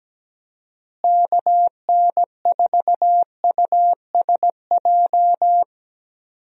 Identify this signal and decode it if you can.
Morse code, heard as a keyed tone, about 17 wpm, 700 Hz. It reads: KN4USJ
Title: USA Call Signs - 6 Characters - Repeated 3x - 17wpm